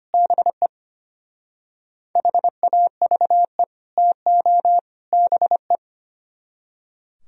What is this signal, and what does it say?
Morse code, heard as a keyed tone, about 30 wpm, 700 Hz. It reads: BE HAVE TO BE
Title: Speed-Racing - Sets of 3 Words 25wpm